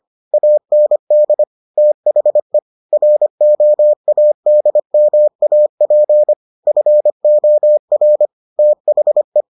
Morse code, heard as a keyed tone, about 25 wpm, 600 Hz. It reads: AND THE ROADMAP FOR THE